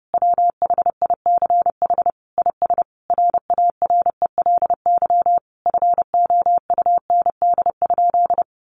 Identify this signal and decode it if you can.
Morse code, heard as a keyed tone, about 30 wpm, 700 Hz. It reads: WHICH IS RARELY FOUND?
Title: Sentences from Top 1400 Words 30wpm